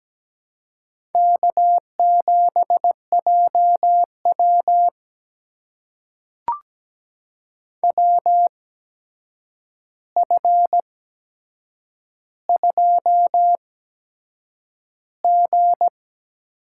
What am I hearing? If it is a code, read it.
Morse code, heard as a keyed tone, about 17 wpm, spaced as for 7 wpm, 700 Hz. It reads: K7JW WF2G
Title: USA Call Signs - 4 Characters - 8x Character Spacing - 17wpm